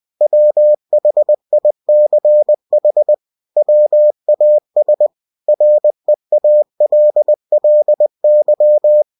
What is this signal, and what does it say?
Morse code, heard as a keyed tone, about 20 wpm, 600 Hz. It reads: WHICH WAS REALLY
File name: cw-books / winnie-the-pooh / winnie-20wpm-page-0040